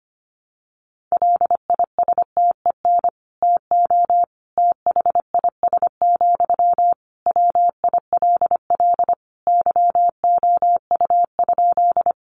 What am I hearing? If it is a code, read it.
Morse code, heard as a keyed tone, about 25 wpm, 700 Hz. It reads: LISTEN TO THIS, WILL YOU?
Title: Sentences from Top 400 Words 25wpm